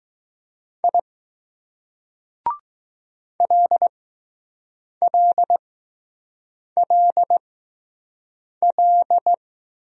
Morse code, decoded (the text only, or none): I LLLL